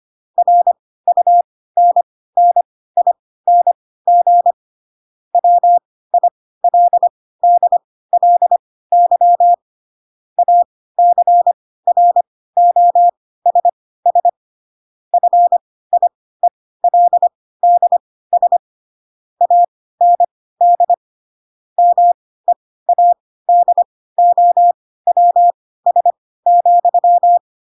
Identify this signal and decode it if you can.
Morse code, heard as a keyed tone, about 25 wpm, spaced as for 16 wpm, 700 Hz. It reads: RUNNING WILDLY ACROSS FIELDS AND MEADOWS,
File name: Pinocchio / Pinocchio ch4 10 25 ik1hge